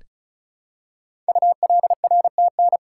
Morse code, heard as a keyed tone, about 36 wpm, 700 Hz. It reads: ULRTN
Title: Code Groups - 5 Letters - 35wpm